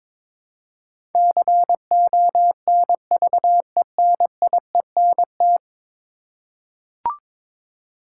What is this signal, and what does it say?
Morse code, heard as a keyed tone, about 22 wpm, 700 Hz. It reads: CONVENIENT
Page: morse-code-ninja-practice-22wpm